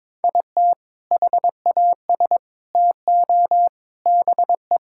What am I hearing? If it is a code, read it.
Morse code, heard as a keyed tone, about 22 wpm, 700 Hz. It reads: IT HAS TO BE
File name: practice / 22wpm / Sentences with 3-Word Phrases - 22wpm